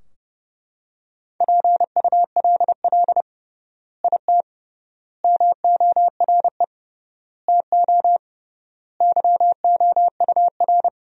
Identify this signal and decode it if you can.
Morse code, heard as a keyed tone, about 30 wpm, 700 Hz. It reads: PULL IT MORE TO YOUR
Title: 3x Word-Spacing - Sentences from Top 400 Words - 30wpm